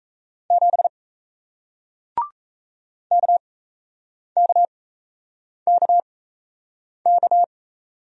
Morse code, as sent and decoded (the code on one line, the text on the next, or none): --... / -.- -.- -.- -.-
7 KKKK